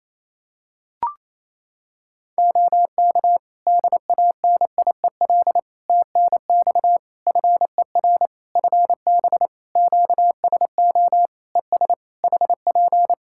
Morse code, heard as a keyed tone, about 28 wpm, 700 Hz. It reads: OK DANIEL TNX FER FB QSO ES HP
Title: Practice QSO - Exchange 4 - Ending - 28wpm